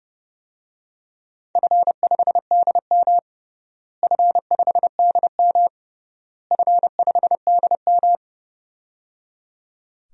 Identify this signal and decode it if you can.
Morse code, heard as a keyed tone, about 30 wpm, 700 Hz. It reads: F5DM F5DM F5DM
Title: DXCC-100 Call Signs - 4 Characters - Repeated 3x - 30wpm